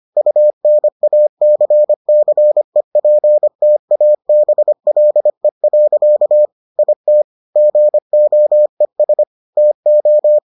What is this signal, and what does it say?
Morse code, heard as a keyed tone, about 25 wpm, 600 Hz. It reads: UNACCEPTABLE. IT GOES TO